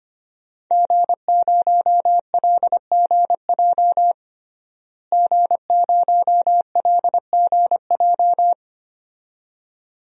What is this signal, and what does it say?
Morse code, heard as a keyed tone, about 25 wpm, 700 Hz. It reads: G0LGJ G0LGJ
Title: DXCC-100 Call Signs - 5 Characters - Repeated 2x - 25wpm